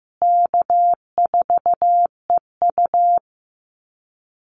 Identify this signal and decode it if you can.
Morse code, heard as a keyed tone, about 15 wpm, 700 Hz. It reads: K4EU
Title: Easy Call Signs Repeated 2x - 15wpm